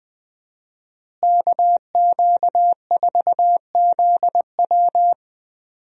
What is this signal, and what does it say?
Morse code, heard as a keyed tone, about 20 wpm, 700 Hz. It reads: KQ4ZW
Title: USA Call Signs - 5 Characters - Repeated 2x - 20wpm